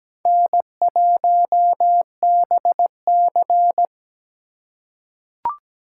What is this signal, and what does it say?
Morse code, heard as a keyed tone, about 17 wpm, 700 Hz. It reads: N1BC
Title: USA Call Signs - 4 Characters - 8x Character Spacing - 17wpm